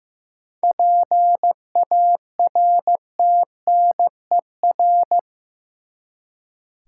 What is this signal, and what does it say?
Morse code, heard as a keyed tone, about 15 wpm, 700 Hz. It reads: PARTNER
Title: Seven Letter Words - 15wpm